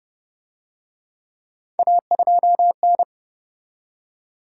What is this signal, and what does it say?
Morse code, heard as a keyed tone, about 30 wpm, 700 Hz. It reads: A2N